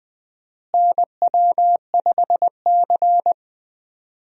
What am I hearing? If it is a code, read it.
Morse code, heard as a keyed tone, about 20 wpm, 700 Hz. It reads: NW5C
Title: USA Call Signs - 4 Characters - Repeated 3x - 20wpm